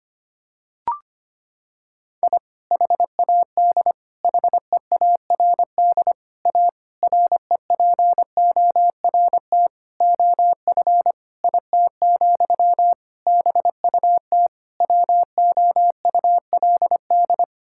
Morse code, heard as a keyed tone, about 25 wpm, 700 Hz. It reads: I HAD HEARD A REPORT OF IT, BUT WOULD
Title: Sentences from Top 1000 Words - Review Part 6 - 25wpm